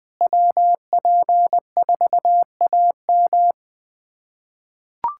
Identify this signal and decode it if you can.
Morse code, heard as a keyed tone, about 20 wpm, 700 Hz. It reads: WP4AM
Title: USA Callsign DE Callsign - 5 Characters - 20wpm